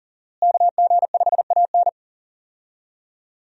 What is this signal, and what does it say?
Morse code, heard as a keyed tone, about 40 wpm, 700 Hz. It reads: KG5AN
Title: USA Call Signs - 5 Characters - Repeated 2x - 40wpm